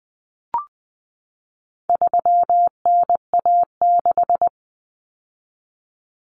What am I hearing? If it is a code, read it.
Morse code, heard as a keyed tone, about 20 wpm, 700 Hz. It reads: 3NA6
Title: Common Cut Numbers - 4-Digit 20wpm